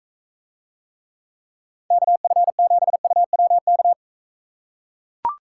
Morse code, heard as a keyed tone, about 44 wpm, 700 Hz. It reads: KF7UWK